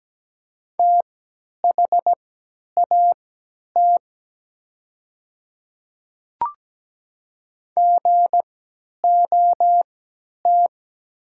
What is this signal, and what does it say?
Morse code, heard as a keyed tone, about 17 wpm, spaced as for 10 wpm, 690 Hz. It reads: THAT GOT
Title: 3x Character Spacing - Top 300 Words - 17wpm